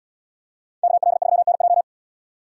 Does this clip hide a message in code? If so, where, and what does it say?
Yes: Morse code from 0.8 to 1.8 s, heard as a keyed tone, about 71 wpm, 700 Hz. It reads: KL3NJ